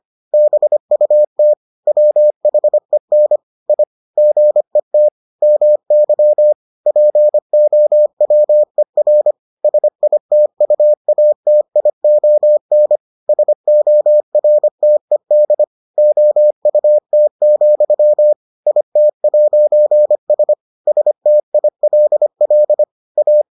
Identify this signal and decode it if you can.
Morse code, heard as a keyed tone, about 25 wpm, 600 Hz. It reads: BUT WHEN I GET MY POWER SITUATION SORTED OUT, IT'S STILL A